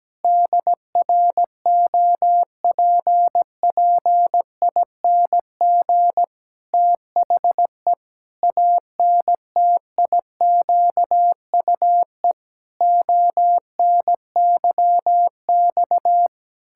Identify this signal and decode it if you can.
Morse code, heard as a keyed tone, about 17 wpm, 700 Hz. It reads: DROPPING THE ANTIQUE ONYX
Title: Pangram Challenge v2 - 17wpm